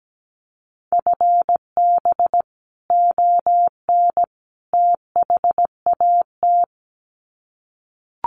Morse code, heard as a keyed tone, about 17 wpm, 700 Hz. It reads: FB ON THAT
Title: Common QSO Phrases 17wpm